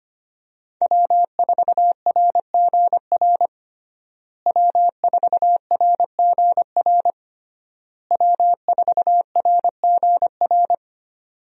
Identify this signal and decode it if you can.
Morse code, heard as a keyed tone, about 25 wpm, 700 Hz. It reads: W4RGR W4RGR W4RGR
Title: USA Call Signs - 5 Characters - Repeated 3x - 25wpm